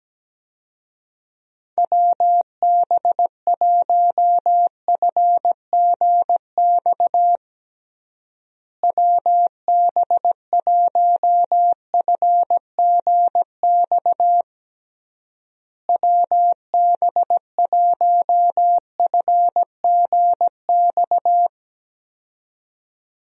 Morse code, heard as a keyed tone, about 17 wpm, 700 Hz. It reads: WB1FGX WB1FGX WB1FGX
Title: USA Call Signs - 6 Characters - Repeated 3x - 17wpm